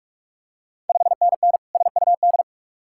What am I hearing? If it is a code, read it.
Morse code, heard as a keyed tone, about 44 wpm, 690 Hz. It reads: 5NN SUD
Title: Ontario QSO Party Exchange 45wpm